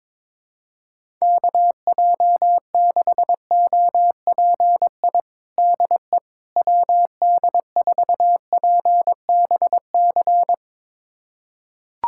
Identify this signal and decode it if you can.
Morse code, heard as a keyed tone, about 22 wpm, 700 Hz. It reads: KJ6OPI DE WD4PBC
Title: USA Callsign DE Callsign - 6 Characters - 22wpm